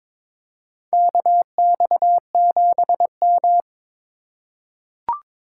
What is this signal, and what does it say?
Morse code, heard as a keyed tone, about 21 wpm, 700 Hz. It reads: KX7M
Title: K1USN SST QSO Practice - 22wpm